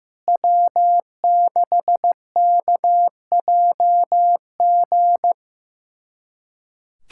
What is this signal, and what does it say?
Morse code, heard as a keyed tone, about 16 wpm, 700 Hz. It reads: W6KJG
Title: USA Callsign DE Callsign - 5 Characters - 15wpm